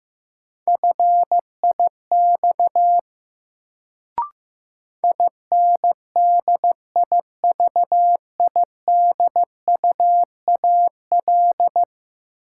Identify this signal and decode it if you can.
Morse code, heard as a keyed tone, about 16 wpm, 700 Hz. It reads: FIX INDIVIDUAL
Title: Sentences from Top 1400 Words - New Word Drill Alt - 15wpm